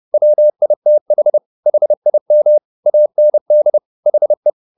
Morse code, heard as a keyed tone, about 30 wpm, 600 Hz. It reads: WITH HIM AND HE
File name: the old man and the sea / the old man and the sea 1 30wpm